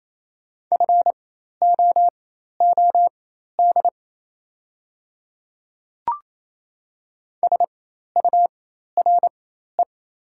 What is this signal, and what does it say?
Morse code, heard as a keyed tone, about 28 wpm, spaced as for 13 wpm, 700 Hz. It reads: FOOD SURE
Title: 4x Character Spacing - Top 300 Words - 28wpm